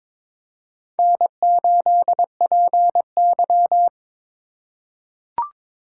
Morse code, heard as a keyed tone, about 22 wpm, 700 Hz. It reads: N8PY
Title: USA Call Signs - 4 Characters - 3x Character Spacing - 22wpm